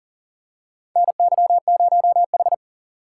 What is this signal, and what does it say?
Morse code, heard as a keyed tone, about 40 wpm, 700 Hz. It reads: NY0H
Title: USA Call Signs - 4 Characters - 8x Character Spacing - 40wpm